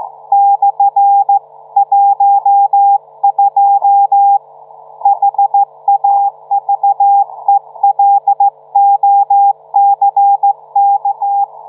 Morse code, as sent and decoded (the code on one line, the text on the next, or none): -..-. .---- ..--- / .... .- ...- . .-.. --- -.-. -.-
/12 HAVELOCK